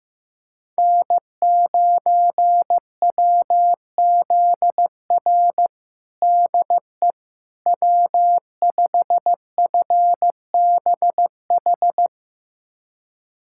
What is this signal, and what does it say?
Morse code, heard as a keyed tone, about 15 wpm, 700 Hz. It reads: N9WZR DE W5FBH
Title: USA Callsign DE Callsign - 5 Characters - 15wpm